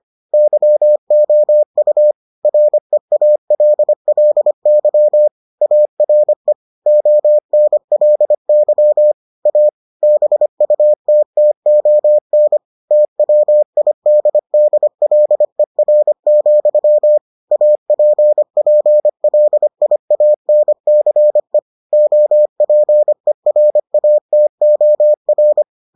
Morse code, heard as a keyed tone, about 25 wpm, 600 Hz. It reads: YOU REALLY ARE ONLY A BUTTON TWIDDLER, APPLIANCE OPERATOR